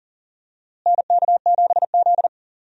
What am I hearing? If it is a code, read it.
Morse code, heard as a keyed tone, about 39 wpm, 700 Hz. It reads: NK7Z
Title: USA Call Signs - 4 Characters - Repeated 3x - 40wpm